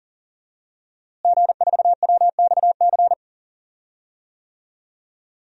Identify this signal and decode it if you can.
Morse code, heard as a keyed tone, about 40 wpm, 700 Hz. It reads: G4WXC
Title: DXCC-100 Call Signs - 5 Characters - Repeated 2x - 40wpm